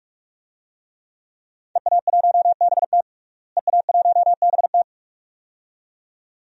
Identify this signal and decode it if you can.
Morse code, heard as a keyed tone, about 42 wpm, 700 Hz. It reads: EA1BT EA1BT
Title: DXCC-100 Call Signs - 5 Characters - Repeated 2x - 45wpm